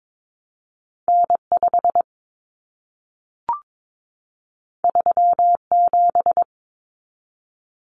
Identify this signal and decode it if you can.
Morse code, heard as a keyed tone, about 22 wpm, 700 Hz. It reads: N5 37
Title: Common Cut Numbers - 2-Digit 22wpm